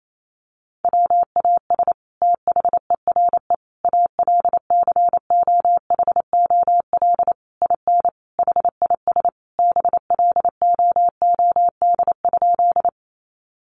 Morse code, heard as a keyed tone, about 28 wpm, 700 Hz. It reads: WAS THERE ALCOHOL IN HIS BLOOD?
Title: Sentences from Top 1400 Words 28wpm